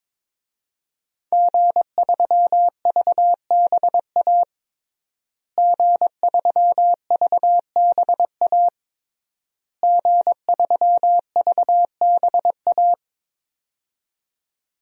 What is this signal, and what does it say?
Morse code, heard as a keyed tone, about 22 wpm, 700 Hz. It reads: G3VBA G3VBA G3VBA
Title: DXCC-100 Call Signs - 5 Characters - Repeated 3x - 22wpm